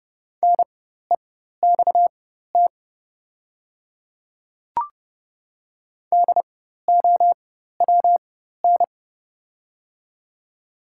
Morse code, heard as a keyed tone, about 29 wpm, spaced as for 14 wpm, 690 Hz. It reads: NEXT DOWN
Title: 4x Character Spacing - Top 300 Words - 30wpm